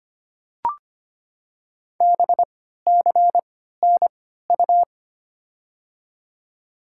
Morse code, heard as a keyed tone, about 25 wpm, spaced as for 14 wpm, 700 Hz. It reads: BCNU
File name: Top 90 QSO Elements - 3x Character Spacing - 25wpm